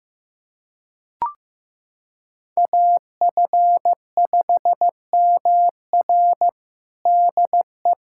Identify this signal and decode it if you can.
Morse code, heard as a keyed tone, about 15 wpm, 700 Hz. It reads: AF5MR DE
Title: USA Callsign DE Callsign - 5 Characters - 15wpm